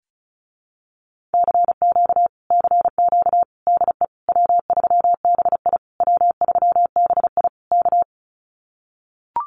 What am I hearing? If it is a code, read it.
Morse code, heard as a keyed tone, about 34 wpm, 700 Hz. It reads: CQ CQ DE W3BI W3BI K
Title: Common QSO Phrases Encore 35wpm